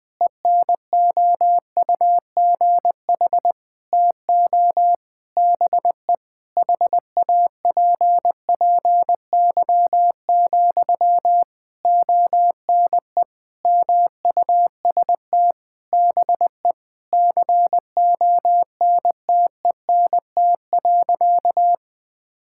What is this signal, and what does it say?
Morse code, heard as a keyed tone, about 21 wpm, 700 Hz. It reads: ENOUGH TO BE HAPPY, ONE MUST BE CONTENT.
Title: Sentences from Top 1000 Words - Review Part 6 - 20wpm